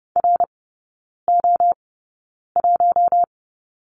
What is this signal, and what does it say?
Morse code, heard as a keyed tone, about 30 wpm, spaced as for 9 wpm, 700 Hz. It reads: RO1
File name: Lesson 16e - Letter c - Three Characters 30wpm